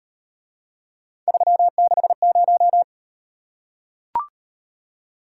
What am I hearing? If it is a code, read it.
Morse code, heard as a keyed tone, about 37 wpm, 700 Hz. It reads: 360